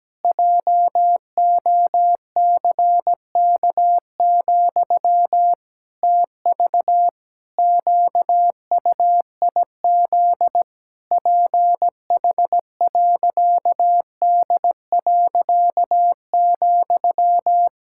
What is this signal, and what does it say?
Morse code, heard as a keyed tone, about 17 wpm, 700 Hz. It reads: JOCK, TV QUIZ PH.D.,
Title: Pangram Challenge v2 - 17wpm